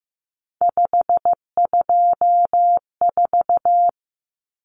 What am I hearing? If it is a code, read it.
Morse code, heard as a keyed tone, about 15 wpm, 700 Hz. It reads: 524
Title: Common Cut Numbers - 3-Digit 15wpm